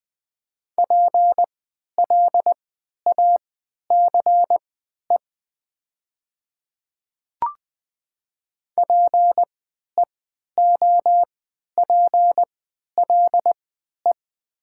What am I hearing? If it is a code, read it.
Morse code, heard as a keyed tone, about 20 wpm, spaced as for 11 wpm, 700 Hz. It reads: PLACE PEOPLE
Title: 3x Character Spacing - Top 300 Words - 20wpm